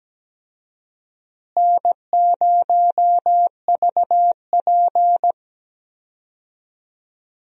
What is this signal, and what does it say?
Morse code, heard as a keyed tone, about 17 wpm, 700 Hz. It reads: N0VP
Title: USA Call Signs - 4 Characters - Repeated 2x - 17wpm